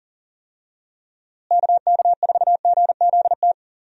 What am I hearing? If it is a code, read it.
Morse code, heard as a keyed tone, about 40 wpm, 700 Hz. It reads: KK4GZT
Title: USA Call Signs - 6 Characters - Repeated 3x - 40wpm